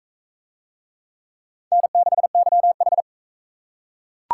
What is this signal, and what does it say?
Morse code, heard as a keyed tone, about 37 wpm, 700 Hz. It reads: N6YH